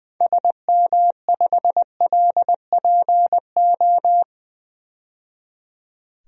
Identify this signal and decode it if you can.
Morse code, heard as a keyed tone, about 20 wpm, 700 Hz. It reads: SM5LPO
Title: DXCC-100 Call Signs - 6 Characters - 20wpm